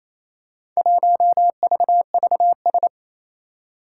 Morse code, heard as a keyed tone, about 27 wpm, 700 Hz. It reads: 1VVS
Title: Four Challenging Letter-Number Combinations 28wpm